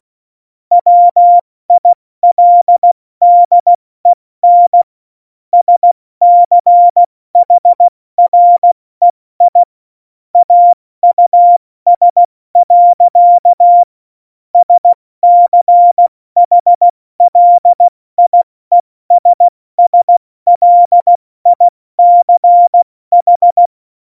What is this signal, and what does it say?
Morse code, heard as a keyed tone, about 16 wpm, 700 Hz. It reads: WILDEN SCHREI AUS. SCHLIESSLICH
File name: Morsekurs / Witze / Witz 11 August 2020, 70 F80